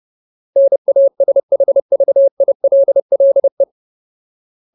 Morse code, heard as a keyed tone, about 30 wpm, 550 Hz. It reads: NASHVILLE
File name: Ultimate Word List / 30WPM Ultimate